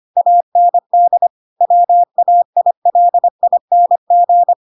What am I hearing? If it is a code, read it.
Morse code, heard as a keyed tone, about 25 wpm, 700 Hz. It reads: AND WAILING
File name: Pinocchio / Pinocchio ch5 25 25 ik1hge